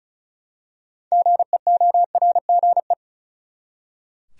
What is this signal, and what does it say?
Morse code, heard as a keyed tone, about 36 wpm, 700 Hz. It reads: GEORGE
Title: USA Common Names - 35wpm